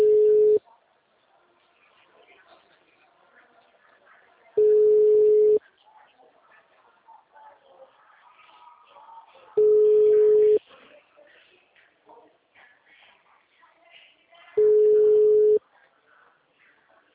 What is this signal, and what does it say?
Ringback tone (Europe, 425 Hz), with speech.
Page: Araras